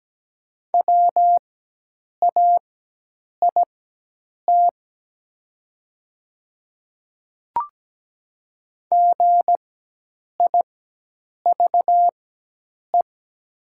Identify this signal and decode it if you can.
Morse code, heard as a keyed tone, about 17 wpm, spaced as for 8 wpm, 700 Hz. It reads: WAIT GIVE
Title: 4x Character Spacing - Top 500 Words - 17wpm